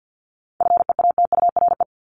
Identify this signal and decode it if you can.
Morse code, heard as a keyed tone, about 50 wpm, 700 Hz. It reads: FEATURE
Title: Top 751-1000 Words 50wpm